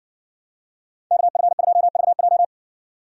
Morse code, heard as a keyed tone, about 57 wpm, 700 Hz. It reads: KF2VJ